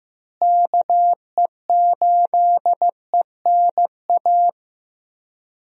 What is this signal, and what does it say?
Morse code, heard as a keyed tone, about 15 wpm, 700 Hz. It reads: KE8ENA
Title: USA Call Signs - 6 Characters - Repeated 3x - 15wpm